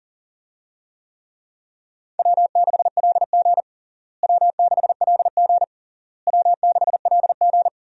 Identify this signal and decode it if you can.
Morse code, heard as a keyed tone, about 40 wpm, 700 Hz. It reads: W6LG W6LG W6LG
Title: USA Call Signs - 4 Characters - Repeated 3x - 40wpm